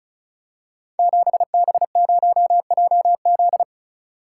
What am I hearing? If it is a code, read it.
Morse code, heard as a keyed tone, about 36 wpm, 700 Hz. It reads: 7B0JZ
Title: Code Groups - 5 Letter:Numbers - 35wpm